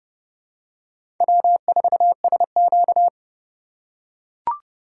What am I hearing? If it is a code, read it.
Morse code, heard as a keyed tone, about 31 wpm, 700 Hz. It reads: W4SQ
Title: USA Call Signs - 4 Characters - 2x Character Spacing - 30wpm